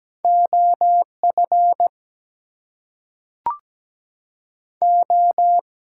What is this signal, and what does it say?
Morse code, heard as a keyed tone, about 17 wpm, 700 Hz. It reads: OF O
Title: Two Letter Words - 17wpm